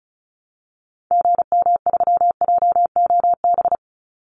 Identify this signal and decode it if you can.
Morse code, heard as a keyed tone, about 34 wpm, 700 Hz. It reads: GM3JOB
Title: Call Signs Repeated 2x - 35wpm